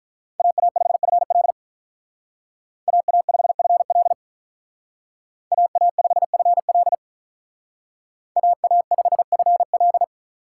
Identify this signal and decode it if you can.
Morse code, heard as a keyed tone, about 43 wpm, 700 Hz. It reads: AA5FL AA5FL AA5FL AA5FL
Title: Speed Racing - USA 2x2 Call Signs 35wpm